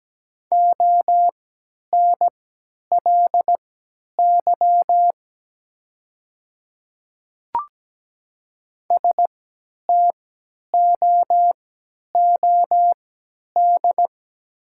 Morse code, heard as a keyed tone, about 17 wpm, spaced as for 10 wpm, 690 Hz. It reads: ONLY STOOD
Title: 3x Character Spacing - Top 500 Words - 17wpm